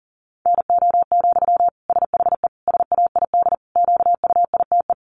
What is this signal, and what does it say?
Morse code, heard as a keyed tone, about 40 wpm, 700 Hz. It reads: NO, SHE SAID QUITE
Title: Sentences from Top 800 Words 40wpm